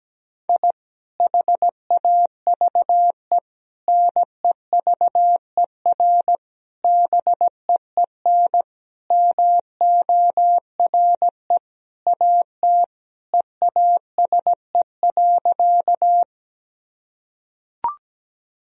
Morse code, heard as a keyed tone, about 17 wpm, 700 Hz. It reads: I HAVE NEVER BEEN MORE AT EASE.
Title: Sentences from Top 1000 Words - Review Part 1 - 17wpm